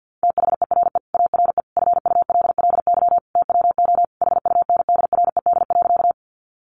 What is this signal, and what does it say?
Morse code, heard as a keyed tone, about 50 wpm, 700 Hz. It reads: THERE ARE FULLY TWO HUNDRED.
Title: Sentences from Top 1300 Words 50wpm